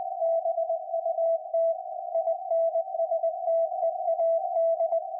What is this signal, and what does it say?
Morse code, heard as a keyed tone, about 20 wpm, 660 Hz. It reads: BUT INSTEAD
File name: IC-7100